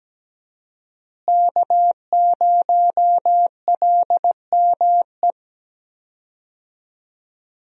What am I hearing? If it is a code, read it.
Morse code, heard as a keyed tone, about 17 wpm, 700 Hz. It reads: K0LME
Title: USA Call Signs - 5 Characters - Repeated 2x - 17wpm